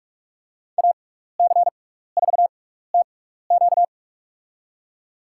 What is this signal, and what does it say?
Morse code, heard as a keyed tone, about 46 wpm, spaced as for 16 wpm, 700 Hz. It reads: AC4TQ
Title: USA Call Signs - 5 Characters - 6x Character Spacing - 45wpm